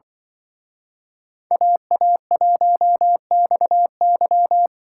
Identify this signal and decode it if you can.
Morse code, heard as a keyed tone, about 24 wpm, 700 Hz. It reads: AA1XY
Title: Speed Racing - USA 2x2 Call Signs 20wpm